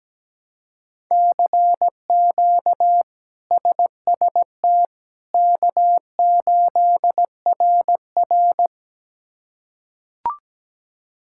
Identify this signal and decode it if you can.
Morse code, heard as a keyed tone, about 17 wpm, 700 Hz. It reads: CQ SST K8RR
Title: K1USN SST QSO Practice - 17wpm